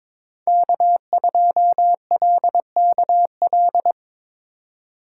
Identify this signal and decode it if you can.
Morse code, heard as a keyed tone, about 26 wpm, 700 Hz. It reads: K2LKL
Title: Speed Racing - USA 1x3 Call Signs 22wpm